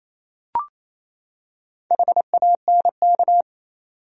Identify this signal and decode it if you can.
Morse code, heard as a keyed tone, about 27 wpm, 700 Hz. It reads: HANK
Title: USA Common Names - Max 4 Letters - 28wpm